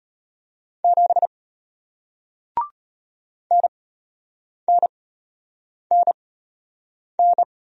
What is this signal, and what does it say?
Morse code, heard as a keyed tone, about 33 wpm, spaced as for 8 wpm, 700 Hz. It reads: 7 NNNN